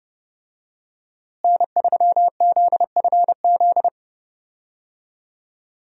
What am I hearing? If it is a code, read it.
Morse code, heard as a keyed tone, about 30 wpm, 700 Hz. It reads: N3ZFZ
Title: USA Call Signs - 5 Characters - Repeated 2x - 30wpm